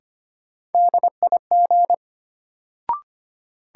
Morse code, heard as a keyed tone, about 25 wpm, 700 Hz. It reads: DIG
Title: Sentences from Top 1600 Words - New Word Drill - 25wpm